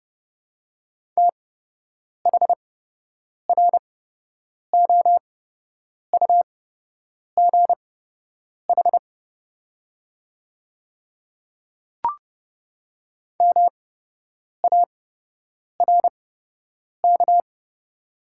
Morse code, heard as a keyed tone, about 30 wpm, spaced as for 8 wpm, 700 Hz. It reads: THROUGH MARK
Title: 8x Character Spacing - Top 300 Words - 30wpm